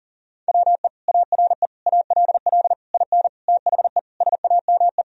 Morse code, heard as a keyed tone, about 40 wpm, 700 Hz. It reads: WE ARE ALL IN THE SAME
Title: Sentences from Top 1000 Words - Review Part 3 - 40wpm